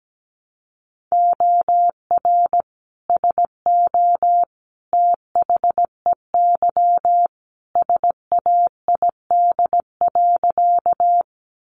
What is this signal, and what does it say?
Morse code, heard as a keyed tone, about 17 wpm, 700 Hz. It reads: OR SO THEY SAID.
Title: Sentences from Top 100 Words 17wpm